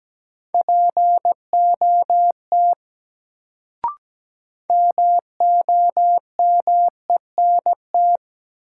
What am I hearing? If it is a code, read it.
Morse code, heard as a keyed tone, about 17 wpm, 700 Hz. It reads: POT MOMENT
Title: Sentences from Top 1000 Words - New Word Drill - 17wpm